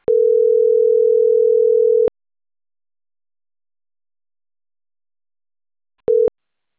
Ringback tone (North America), 440 + 480 Hz.